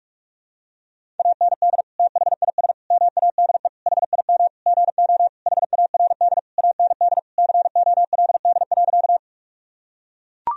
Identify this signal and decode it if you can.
Morse code, heard as a keyed tone, about 41 wpm, 700 Hz. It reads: AND THIS MADE HIM GO HARD AND COLD.
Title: Sentences from Top 1000 Words - Review Part 1 - 45wpm